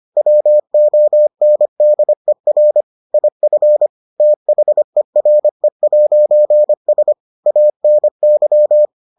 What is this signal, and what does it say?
Morse code, heard as a keyed tone, about 25 wpm, 600 Hz. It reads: WONDER IF THERE'S ANY